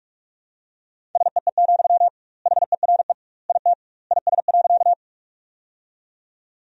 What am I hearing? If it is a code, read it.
Morse code, heard as a keyed tone, about 41 wpm, 700 Hz. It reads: SEE, HERE IT IS.
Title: 2x Word-Spacing - Sentences from Top 200 Words - 45wpm